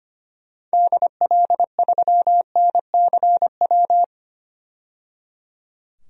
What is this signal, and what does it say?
Morse code, heard as a keyed tone, about 25 wpm, 700 Hz. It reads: DL3NCW
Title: DXCC-100 Call Signs - 6 Characters - 25wpm